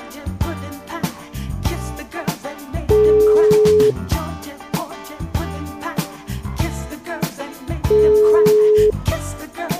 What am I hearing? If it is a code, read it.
Ringback tone (Europe, 425 Hz).